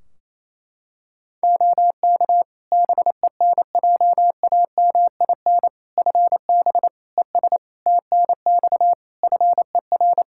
Morse code, heard as a keyed tone, about 28 wpm, 700 Hz. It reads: OK BENJAMIN FB ES TNX FER
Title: K1USN SST QSO Practice 28wpm